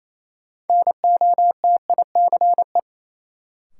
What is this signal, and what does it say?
Morse code, heard as a keyed tone, about 28 wpm, 700 Hz. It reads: NOTICE
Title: Sentences from Top 400 Words - New Word Drill - 28wpm